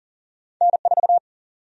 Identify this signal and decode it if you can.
Morse code, heard as a keyed tone, about 40 wpm, 700 Hz. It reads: N4
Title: USA Call Sign Prefixes - 40wpm